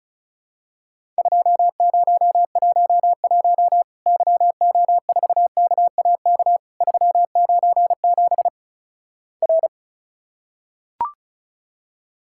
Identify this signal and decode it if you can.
Morse code, heard as a keyed tone, about 36 wpm, 700 Hz. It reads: 2011 YO4KAK 397 R
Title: Worked All Europe DX Contest - QTC Exchange - 35wpm